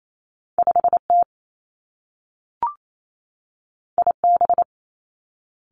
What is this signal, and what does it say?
Morse code, heard as a keyed tone, about 28 wpm, 700 Hz. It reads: 5 T I B
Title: Two Challenging Letter-Number Combinations 28wpm